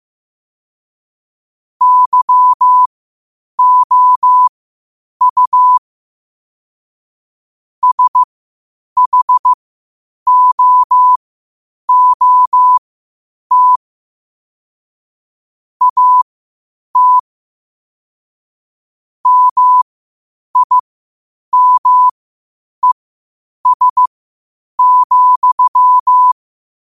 Morse code, heard as a keyed tone, about 15 wpm, spaced as for 8 wpm, 1000 Hz. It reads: YOU SHOOT AT MIMES,
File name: 8wpm